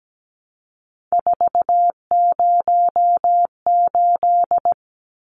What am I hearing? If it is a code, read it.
Morse code, heard as a keyed tone, about 17 wpm, 700 Hz. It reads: 408
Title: Numbers - 3-Digit 17wpm